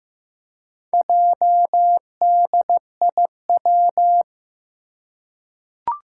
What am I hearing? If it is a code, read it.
Morse code, heard as a keyed tone, about 15 wpm, 700 Hz. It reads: JDIW
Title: Four Challenging Letter-Number Combinations 15wpm